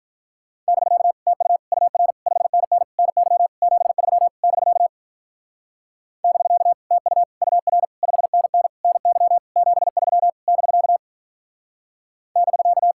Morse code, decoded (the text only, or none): <BK> TU UR 5NN NY 73 <BK> <BK> TU UR 5NN NY 73 <BK> <BK>